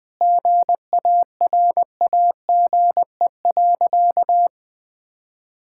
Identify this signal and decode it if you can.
Morse code, heard as a keyed tone, about 21 wpm, 700 Hz. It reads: GARAGE.
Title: Sentences from Top 1600 Words 20wpm